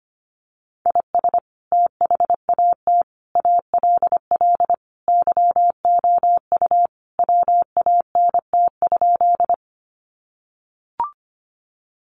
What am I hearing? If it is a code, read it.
Morse code, heard as a keyed tone, about 25 wpm, 700 Hz. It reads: IS THAT ALL YOU WANT?
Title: Sentences from Top 200 Words 25wpm